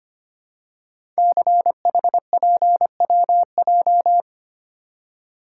Morse code, heard as a keyed tone, about 25 wpm, 700 Hz. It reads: CHPWJ
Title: Code Groups - 5 Letters - 25wpm